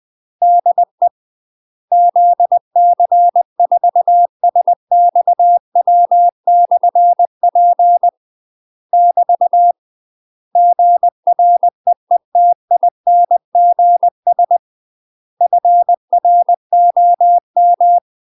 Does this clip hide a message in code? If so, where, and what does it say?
Yes: Morse code from 0.4 to 18.0 s, heard as a keyed tone, about 20 wpm, 700 Hz. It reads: DE ZC4SXW/P <BT> GREETINGS FROM